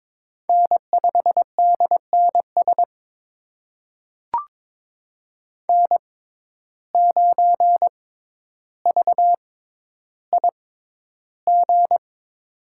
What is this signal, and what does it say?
Morse code, heard as a keyed tone, about 22 wpm, spaced as for 11 wpm, 700 Hz. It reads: N5DNS N9VIG
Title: USA Call Signs - 5 Characters - 6x Character Spacing - 22wpm